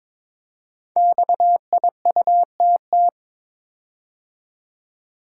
Morse code, heard as a keyed tone, about 23 wpm, 700 Hz. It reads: XIUTT
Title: Code Groups - 5 Letters - 22wpm